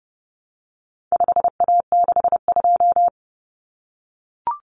Morse code, heard as a keyed tone, about 29 wpm, 700 Hz. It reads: 5A62
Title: Common Cut Numbers - 4-Digit 30wpm